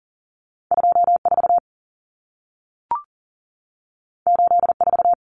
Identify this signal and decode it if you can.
Morse code, heard as a keyed tone, about 40 wpm, 700 Hz. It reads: 24 84
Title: Numbers - 2-Digit 40wpm